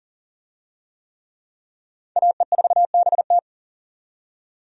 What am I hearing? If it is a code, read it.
Morse code, heard as a keyed tone, about 37 wpm, 700 Hz. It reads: AE4BT
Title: USA Call Signs - 5 Characters - 40wpm